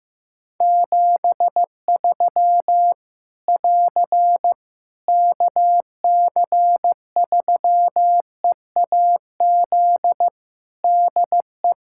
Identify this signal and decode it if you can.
Morse code, heard as a keyed tone, about 15 wpm, 700 Hz. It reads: 73 <AR> KC3EAZ DE
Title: Practice QSO - Exchange 4 - Ending - 15wpm